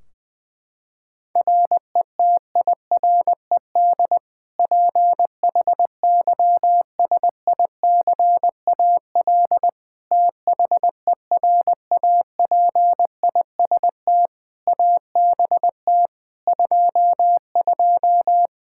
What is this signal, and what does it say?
Morse code, heard as a keyed tone, about 20 wpm, 700 Hz. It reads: RETIRED PHYSICAL THERAPIST ABT 22
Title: Practice QSO - Exchange 3 - More Info - 20wpm